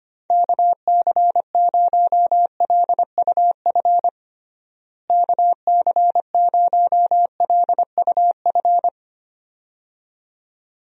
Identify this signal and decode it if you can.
Morse code, heard as a keyed tone, about 25 wpm, 700 Hz. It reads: KC0LUF KC0LUF
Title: USA Call Signs - 6 Characters - Repeated 2x - 25wpm